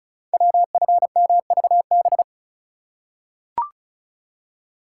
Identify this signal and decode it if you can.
Morse code, heard as a keyed tone, about 36 wpm, 700 Hz. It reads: WFMVB
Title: Code Groups - 5 Letters - 35wpm